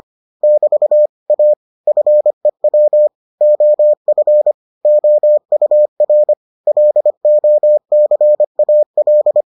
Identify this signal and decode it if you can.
Morse code, heard as a keyed tone, about 25 wpm, 600 Hz. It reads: <BT> A FEW OF OUR LOCAL